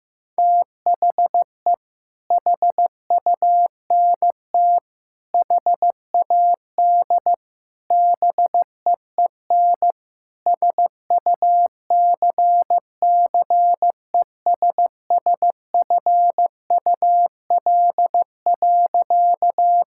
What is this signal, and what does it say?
Morse code, heard as a keyed tone, about 15 wpm, 700 Hz. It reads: THE HUNT HAD BEEN SUCCESSFUL.
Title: Sentences from Top 1600 Words 15wpm